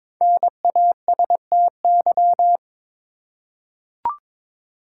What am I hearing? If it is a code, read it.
Morse code, heard as a keyed tone, about 22 wpm, 700 Hz. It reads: NASTY
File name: 5 Letter Words - First 3 Letters Given - 22wpm